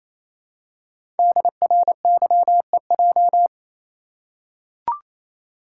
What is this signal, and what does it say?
Morse code, heard as a keyed tone, about 29 wpm, 700 Hz. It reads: DRYEJ
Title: Code Groups - 5 Letters - 28wpm